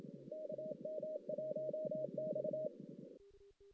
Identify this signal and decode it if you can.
Morse code, heard as a keyed tone, about 26 wpm, 620 Hz, with radio noise. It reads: KM1X
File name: 2016 JAN CW NAQP / 2016 Jan naqp k5ot - 0-5